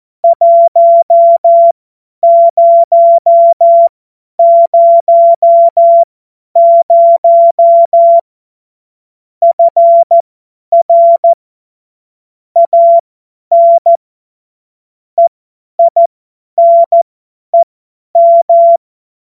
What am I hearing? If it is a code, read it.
Morse code, heard as a keyed tone, about 14 wpm, spaced as for 10 wpm, 680 Hz. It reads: 1000 FR AN EINEM
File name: Morsekurs / Witze / Witz 08. August 2023 50 F70